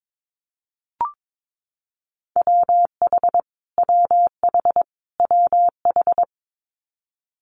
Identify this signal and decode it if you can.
Morse code, heard as a keyed tone, about 22 wpm, 700 Hz. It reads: WH WH WH
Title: Multi-letter Phonemes Repeated 3x 22wpm